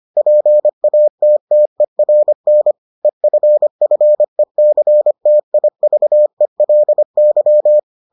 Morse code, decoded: PATTERN EFFECTIVELY

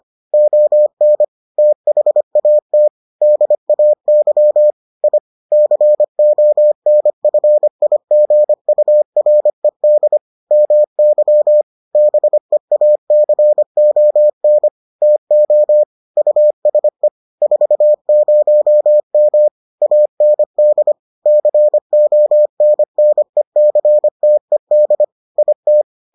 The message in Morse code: ON THAT DAY I CONFIGURED MY BEACON TO USE 40M AND CONNECTED IT

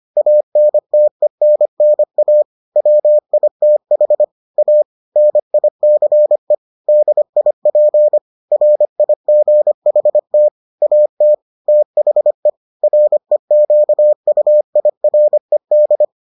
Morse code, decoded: ANTENNA WITH A NICE DIP RIGHT AT THE REQUIRED

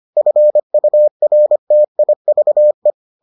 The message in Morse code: FURTIVE

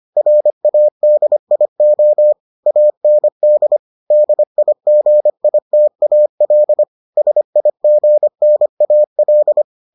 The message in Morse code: RADIO AND DIGITAL SIGNAL

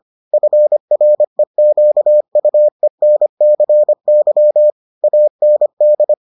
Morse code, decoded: FREQUENCY AND

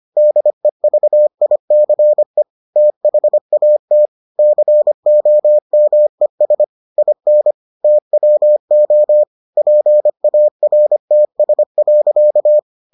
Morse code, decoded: DEVICE THAT COMES IN TWO PARTS.

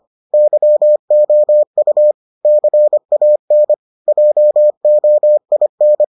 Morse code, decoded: YOU CAN JOIN

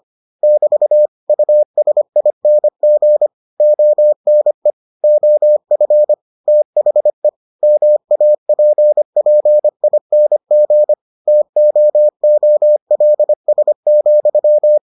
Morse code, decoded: <BT> USING ONE OF THE MAPPING TOOLS,

